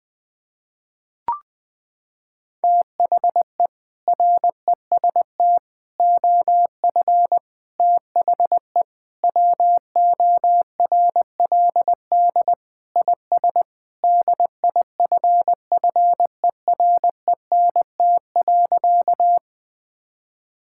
Morse code, heard as a keyed tone, about 20 wpm, 700 Hz. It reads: THE REST OF THE WORLD IS DIFFERENT.
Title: Sentences with 4-Word Phrases - 20wpm